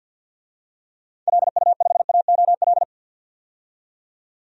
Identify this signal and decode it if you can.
Morse code, heard as a keyed tone, about 52 wpm, 700 Hz. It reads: LU5AOL
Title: DXCC-100 Call Signs - 6 Characters - 50wpm